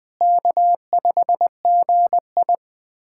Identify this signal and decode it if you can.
Morse code, heard as a keyed tone, about 20 wpm, 700 Hz. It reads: K5GI
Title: USA Callsign DE Callsign - 4 Characters - 20wpm